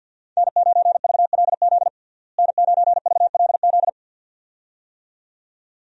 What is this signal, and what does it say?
Morse code, heard as a keyed tone, about 51 wpm, 700 Hz. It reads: N9VLZ N9VLZ
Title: USA Call Signs - 5 Characters - Repeated 2x - 50wpm